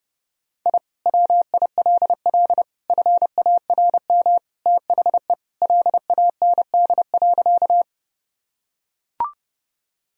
Morse code, heard as a keyed tone, about 30 wpm, 700 Hz. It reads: I WILL FARM THE LAND.
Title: Sentences from Top 1000 Words - Review Part 2 - 30wpm